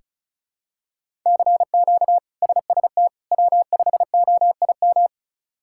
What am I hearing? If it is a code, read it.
Morse code, heard as a keyed tone, about 34 wpm, 700 Hz. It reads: CQ SST W5OIM
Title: K1USN SST QSO Practice - 35wpm